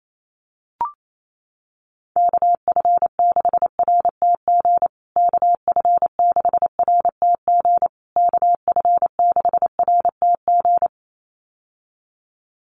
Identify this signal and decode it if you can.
Morse code, heard as a keyed tone, about 28 wpm, 700 Hz. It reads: KF6RTG KF6RTG KF6RTG
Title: USA Call Signs Repeated 3x 28wpm